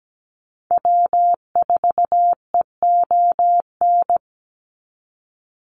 Morse code, heard as a keyed tone, about 17 wpm, 700 Hz. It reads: W4EON
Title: NA SOTA Activator Callsigns 17wpm